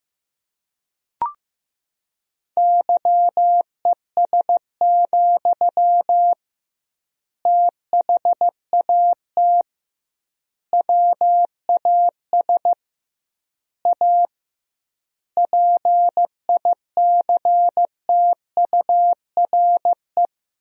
Morse code, heard as a keyed tone, about 15 wpm, 700 Hz. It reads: YES, THAT WAS A PICTURE
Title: 2x Word-Spacing - Sentences from Top 500 Words - 15wpm